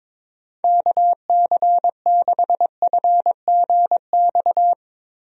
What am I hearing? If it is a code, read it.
Morse code, heard as a keyed tone, about 22 wpm, 700 Hz. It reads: KC6FGX
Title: USA Call Signs - 6 Characters - Repeated 3x - 22wpm